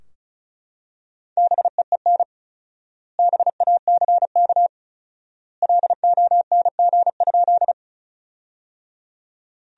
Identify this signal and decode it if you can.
Morse code, heard as a keyed tone, about 36 wpm, 700 Hz. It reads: BEEN BACK LONG?
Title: 4x Word-Spacing - Sentences from Top 200 Words - 35wpm